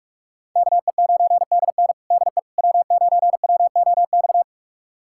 Morse code, heard as a keyed tone, about 41 wpm, 700 Hz. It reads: KE9DN DE W9WOX
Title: USA Callsign DE Callsign - 5 Characters - 45wpm